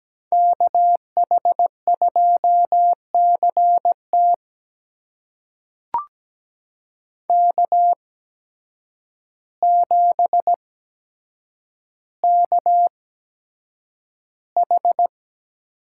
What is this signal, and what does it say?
Morse code, heard as a keyed tone, about 17 wpm, spaced as for 8 wpm, 700 Hz. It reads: KH2CT K7KH
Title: USA Call Signs - 5 Characters - 8x Character Spacing - 17wpm